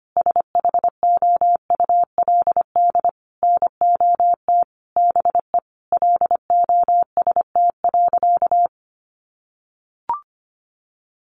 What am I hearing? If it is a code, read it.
Morse code, heard as a keyed tone, about 25 wpm, 700 Hz. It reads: SHOULD NOT BE LOST.
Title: Sentences from Top 1400 Words 25wpm